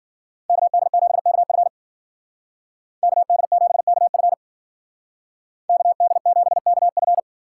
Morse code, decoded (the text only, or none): KD7KF KD7KF KD7KF